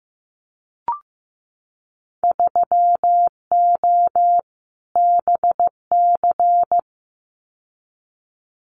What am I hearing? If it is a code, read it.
Morse code, heard as a keyed tone, about 15 wpm, 700 Hz. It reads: 3O BC
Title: Winter Field Day Exchanges Encore 15wpm